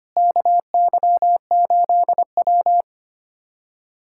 Morse code, heard as a keyed tone, about 25 wpm, 700 Hz. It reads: KY8W
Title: USA Callsign DE Callsign - 4 Characters - 25wpm